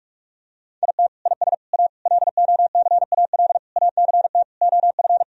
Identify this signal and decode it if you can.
Morse code, heard as a keyed tone, about 46 wpm, 700 Hz. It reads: IT IS A LOCAL ACT OF
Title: Sentences from Top 1000 Words - Review Part 3 - 45wpm